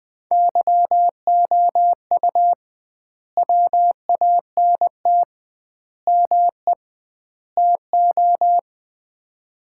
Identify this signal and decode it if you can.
Morse code, heard as a keyed tone, about 20 wpm, 700 Hz. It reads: YOU WANT ME TO
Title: 2x Word-Spacing - Sets of 4 Words - 20wpm